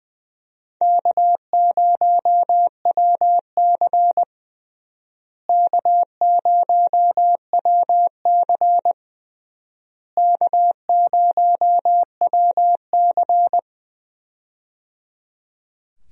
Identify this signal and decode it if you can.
Morse code, heard as a keyed tone, about 20 wpm, 700 Hz. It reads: K0WC K0WC K0WC
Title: USA Call Signs - 4 Characters - Repeated 3x - 20wpm